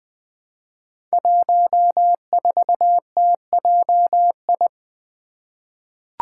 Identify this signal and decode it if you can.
Morse code, heard as a keyed tone, about 20 wpm, 700 Hz. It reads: 14TJI
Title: Five Challenging Letter-Number Combinations 20wpm